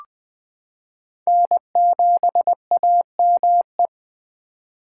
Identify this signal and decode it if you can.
Morse code, heard as a keyed tone, about 20 wpm, 700 Hz. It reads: N7AME
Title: K1USN SST QSO Practice - 20wpm